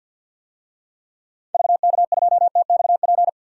Morse code, heard as a keyed tone, about 51 wpm, 700 Hz. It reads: VK2TXP
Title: DXCC-100 Call Signs - 6 Characters - 50wpm